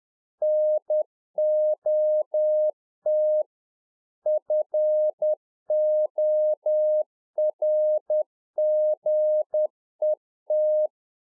Morse code, heard as a keyed tone, about 10 wpm, 610 Hz. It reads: NOT FORGET
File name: 2011-04-mp3 / RICK JJ0MPI 10WPM